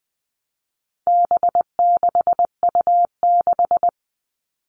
Morse code, heard as a keyed tone, about 20 wpm, 700 Hz. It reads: B6U6
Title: All Cut Numbers - 4-Digit 20wpm